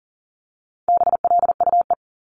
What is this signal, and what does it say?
Morse code, heard as a keyed tone, about 41 wpm, 700 Hz. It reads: BLUE
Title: Multi-letter Phonemes Repeated 3x 40wpm